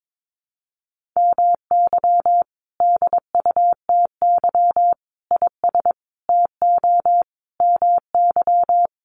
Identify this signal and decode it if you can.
Morse code, heard as a keyed tone, about 22 wpm, 700 Hz. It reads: MY DUTY IS TO MY
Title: Sentences from Top 1500 Words 22wpm